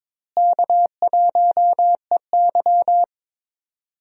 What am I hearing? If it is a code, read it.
Morse code, heard as a keyed tone, about 22 wpm, 700 Hz. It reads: K1EY
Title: USA Call Signs - 4 Characters - Repeated 2x - 22wpm